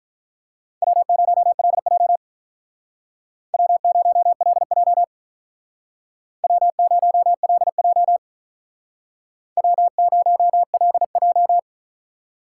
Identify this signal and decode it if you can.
Morse code, heard as a keyed tone, about 45 wpm, 700 Hz. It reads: W0LJ W0LJ W0LJ W0LJ